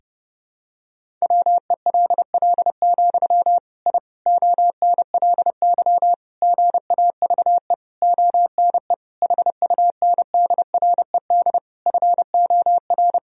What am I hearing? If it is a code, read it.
Morse code, heard as a keyed tone, about 30 wpm, 700 Hz. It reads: WELL, I ONLY GAVE ONE HUNDRED FOR
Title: Sentences from Top 1000 Words - Review Part 2 - 30wpm